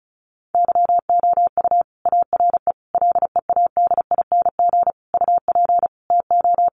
Morse code, heard as a keyed tone, about 34 wpm, 700 Hz. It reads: YOU ARE LEADING UP TO
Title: Sentences from Top 1100 Words 35wpm